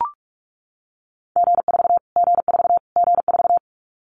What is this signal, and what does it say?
Morse code, heard as a keyed tone, about 44 wpm, 700 Hz. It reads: G4 G4 G4
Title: Callsign Prefixes for Top 25 DXCC - Repeated 3x - 45wpm